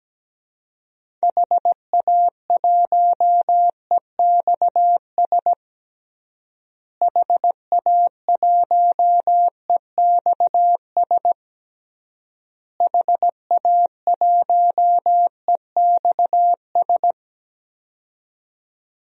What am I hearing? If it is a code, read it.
Morse code, heard as a keyed tone, about 17 wpm, 700 Hz. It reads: HA1EXS HA1EXS HA1EXS
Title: DXCC-100 Call Signs - 6 Characters - Repeated 3x - 17wpm